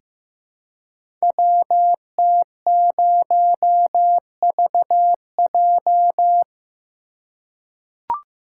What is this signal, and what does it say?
Morse code, heard as a keyed tone, about 15 wpm, 700 Hz. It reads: WT0VJ
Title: Code Groups - 5 Letter:Numbers - 15wpm